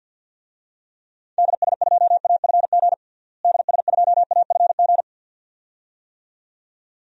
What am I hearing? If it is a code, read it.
Morse code, heard as a keyed tone, about 50 wpm, 700 Hz. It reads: DS2AFG DS2AFG
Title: DXCC-100 Call Signs - 6 Characters - Repeated 2x - 50wpm